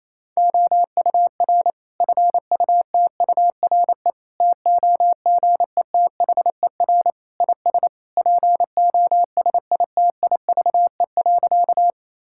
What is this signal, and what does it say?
Morse code, heard as a keyed tone, about 28 wpm, 700 Hz. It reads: OUR FUTURE TOGETHER IS POSITIVE.
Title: Sentences from Top 1000 Words - Review Part 5 - 28wpm